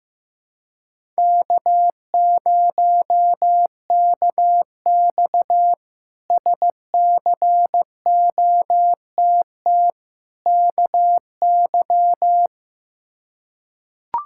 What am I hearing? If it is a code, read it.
Morse code, heard as a keyed tone, about 15 wpm, 700 Hz. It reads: K0KX SCOTT KY
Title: K1USN SST QSO Practice - 15wpm